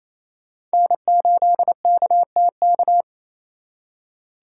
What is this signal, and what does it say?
Morse code, heard as a keyed tone, about 34 wpm, 700 Hz. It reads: N8KTK
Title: Speed Racing - USA Call Signs 28wpm